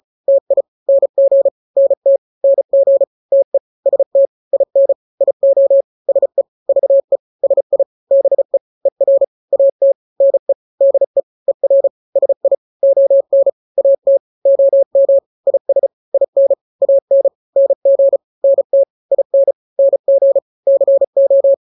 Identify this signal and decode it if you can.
Morse code, heard as a keyed tone, about 34 wpm, 550 Hz. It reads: TI NG NT NG TE ST IN IO SE VE SI BE ER AT NE DE ER SI ON AT OM IS IN AN NG NT IN NG CO
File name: n-grams / 2-grams 35 30 WPM